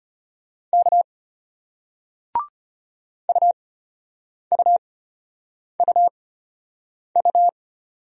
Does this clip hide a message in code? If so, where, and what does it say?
Yes: Morse code from 0.7 to 7.5 s, heard as a keyed tone, about 32 wpm, spaced as for 8 wpm, 700 Hz. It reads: K UUUU